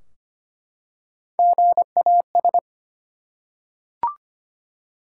Morse code, heard as a keyed tone, about 25 wpm, 700 Hz. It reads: GAS
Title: Three Letter Words - 25wpm